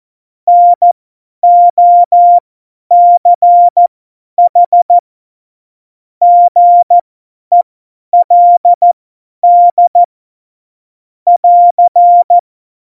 Morse code, heard as a keyed tone, about 14 wpm, spaced as for 10 wpm, 700 Hz. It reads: NOCH GELD <AR>
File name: Morsekurs / Witze / Witz 11. September 2018, 50 F70